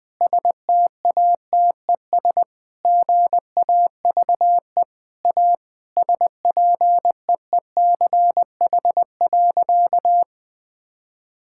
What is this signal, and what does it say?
Morse code, heard as a keyed tone, about 20 wpm, 700 Hz. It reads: STATES GAVE A SPEECH.
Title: Sentences with 5-Word Phrases - 20wpm